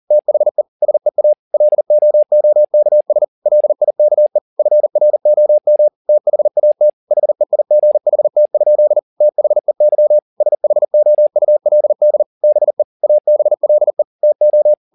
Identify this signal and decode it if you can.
Morse code, heard as a keyed tone, about 40 wpm, 600 Hz. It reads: THE SEA LOOKS LIKE FROM THAT HEIGHT? THEY SHOULD BE ABLE TO
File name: the old man and the sea 2 40wpm